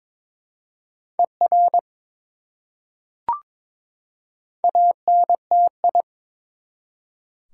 Morse code, heard as a keyed tone, about 22 wpm, 700 Hz. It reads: ER ANTI
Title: Prefixes and Suffixes 22wpm